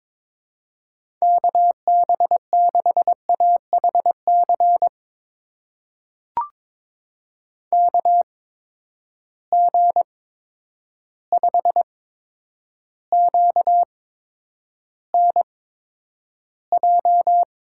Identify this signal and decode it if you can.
Morse code, heard as a keyed tone, about 22 wpm, spaced as for 9 wpm, 700 Hz. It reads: KB6AHC KG5QNJ
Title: USA Call Signs - 6 Characters - 8x Character Spacing - 22wpm